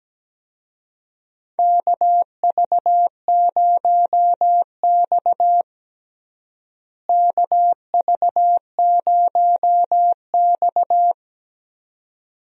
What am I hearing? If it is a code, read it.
Morse code, heard as a keyed tone, about 17 wpm, 700 Hz. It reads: KV0X KV0X
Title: USA Call Signs - 4 Characters - Repeated 2x - 17wpm